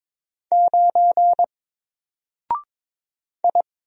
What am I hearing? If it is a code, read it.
Morse code, heard as a keyed tone, about 22 wpm, 700 Hz. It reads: 9 I
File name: Single-Letter-Number-22wpm